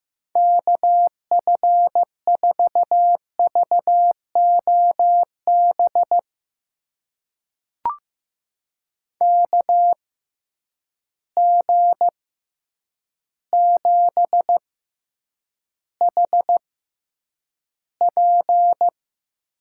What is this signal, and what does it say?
Morse code, heard as a keyed tone, about 15 wpm, spaced as for 8 wpm, 700 Hz. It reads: KF4VOB KG7HP
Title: USA Call Signs - 6 Characters - 6x Character Spacing - 15wpm